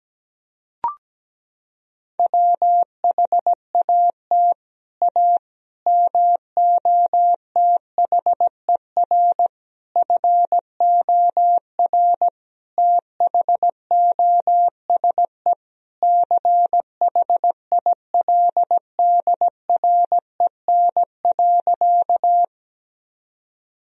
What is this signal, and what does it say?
Morse code, heard as a keyed tone, about 17 wpm, 700 Hz. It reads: WHAT A MOTHER FOR THOSE CHILDREN.
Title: Sentences from Top 1000 Words - Review Part 1 - 17wpm